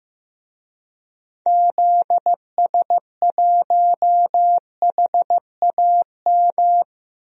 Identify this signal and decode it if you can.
Morse code, heard as a keyed tone, about 15 wpm, 700 Hz. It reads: ZS1HAM
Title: DXCC-100 Call Signs - 6 Characters - 15wpm